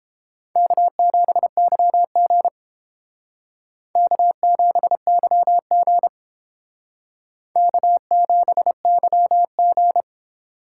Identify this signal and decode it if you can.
Morse code, heard as a keyed tone, about 30 wpm, 700 Hz. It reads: K7YG K7YG K7YG